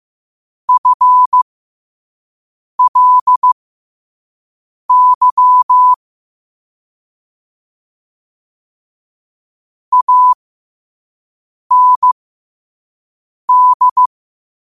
Morse code, heard as a keyed tone, about 15 wpm, spaced as for 5 wpm, 1000 Hz. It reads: FLY AND
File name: qotdfiles / 5wpm